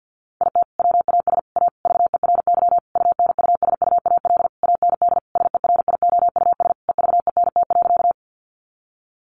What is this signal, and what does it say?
Morse code, heard as a keyed tone, about 50 wpm, 700 Hz. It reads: IT WAS A VERY UNUSUAL AND SERIOUS EVENT.
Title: Sentences from Top 1400 Words 50wpm